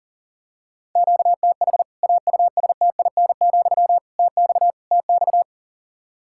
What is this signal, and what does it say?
Morse code, heard as a keyed tone, about 40 wpm, 700 Hz. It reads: QTH AUSTIN, TX TX
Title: K1USN SST QSO Practice 40wpm